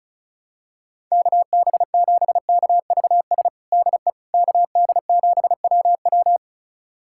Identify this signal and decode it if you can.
Morse code, heard as a keyed tone, about 34 wpm, 700 Hz. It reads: KB7KVS DE KD7WW
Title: USA Callsign DE Callsign 35wpm